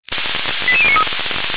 Five-tone selective call (ZVEI), heard as quick runs of five tones, tones of about 80 ms. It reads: 88093